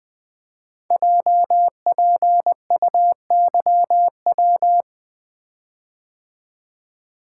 Morse code, heard as a keyed tone, about 20 wpm, 700 Hz. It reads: JPUYW
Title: Code Groups - 5 Letters - 20wpm